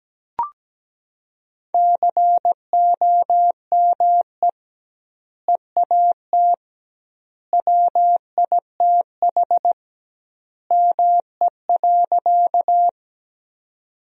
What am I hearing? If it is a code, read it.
Morse code, heard as a keyed tone, about 17 wpm, 700 Hz. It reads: COME EAT WITH ME.
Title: 2x Word-Spacing - Sentences from Top 300 Words - 17wpm